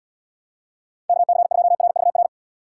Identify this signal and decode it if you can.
Morse code, heard as a keyed tone, about 78 wpm, 700 Hz. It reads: KC2RVR